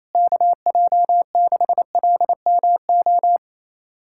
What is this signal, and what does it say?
Morse code, heard as a keyed tone, about 27 wpm, 700 Hz. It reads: KJ6LMO
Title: USA Callsign DE Callsign - 6 Characters - 28wpm